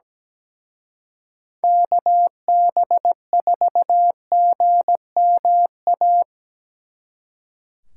Morse code, heard as a keyed tone, about 20 wpm, 700 Hz. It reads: KB4GMA
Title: Speed Racing - USA 2x3 Call Signs 17wpm